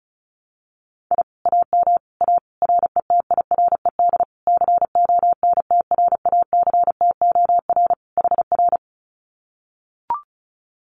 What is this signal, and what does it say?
Morse code, heard as a keyed tone, about 34 wpm, 700 Hz. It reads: I AM A RETIRED CONTRACTOR HR
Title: Common QSO Phrases Encore 35wpm